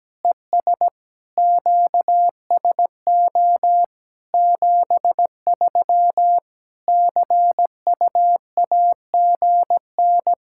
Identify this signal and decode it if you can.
Morse code, heard as a keyed tone, about 17 wpm, 700 Hz. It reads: ES QSO 73 CU AGN
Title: Practice QSO - Exchange 4 - Ending - 17wpm